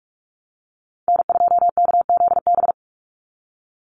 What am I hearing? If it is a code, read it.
Morse code, heard as a keyed tone, about 44 wpm, 700 Hz. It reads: N2KZB
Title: USA Call Signs 45wpm